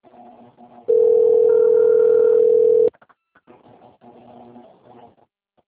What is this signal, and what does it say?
Ringback tone (North America), 440 + 480 Hz.